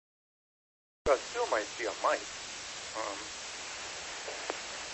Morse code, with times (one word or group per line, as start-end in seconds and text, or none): none